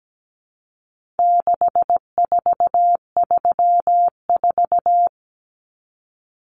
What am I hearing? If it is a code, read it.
Morse code, heard as a keyed tone, about 17 wpm, 700 Hz. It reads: 6434
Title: Numbers - 4-Digit 17wpm